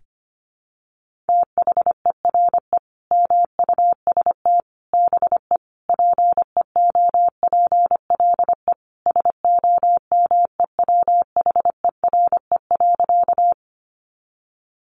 Morse code, heard as a keyed tone, about 25 wpm, 700 Hz. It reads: THERE MUST BE PEOPLE SOMEWHERE.
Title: Sentences from Top 1400 Words 25wpm